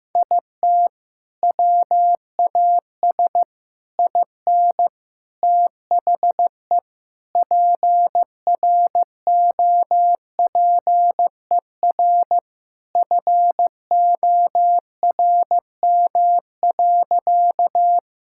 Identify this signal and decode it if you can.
Morse code, heard as a keyed tone, about 15 wpm, 700 Hz. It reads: IT WAS IN THE PROPER FORM.
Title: Sentences from Top 1000 Words - Review Part 5 - 15wpm